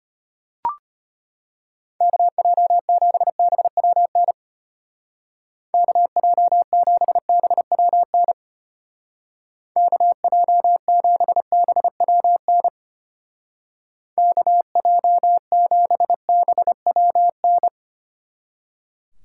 Morse code, heard as a keyed tone, about 32 wpm, 700 Hz. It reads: KJ7BWN KJ7BWN KJ7BWN KJ7BWN